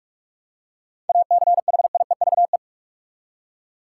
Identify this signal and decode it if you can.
Morse code, heard as a keyed tone, about 47 wpm, 700 Hz. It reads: ACHIEVE
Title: Seven Letter Words - 45wpm